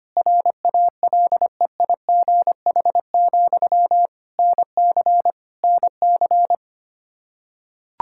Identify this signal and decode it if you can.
Morse code, heard as a keyed tone, about 25 wpm, 700 Hz. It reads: RALEIGH, NC NC
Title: K1USN SST QSO Practice 25wpm